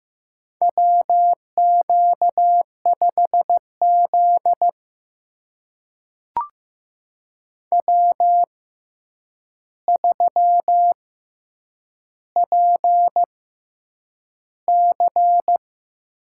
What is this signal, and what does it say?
Morse code, heard as a keyed tone, about 15 wpm, spaced as for 8 wpm, 700 Hz. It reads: WQ5Z W3PC
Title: USA Call Signs - 4 Characters - 6x Character Spacing - 15wpm